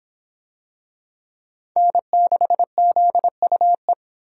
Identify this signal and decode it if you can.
Morse code, heard as a keyed tone, about 24 wpm, 700 Hz. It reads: N6ZUE